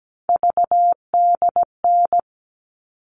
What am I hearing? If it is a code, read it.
Morse code, heard as a keyed tone, about 17 wpm, 700 Hz. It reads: VDN
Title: All Cut Numbers - 3-Digit 17wpm